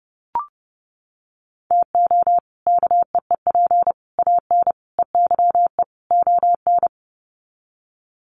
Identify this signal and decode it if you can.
Morse code, heard as a keyed tone, about 30 wpm, 700 Hz. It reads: TO KEEP AN EYE ON
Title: Sets of 5 Words Encore 30wpm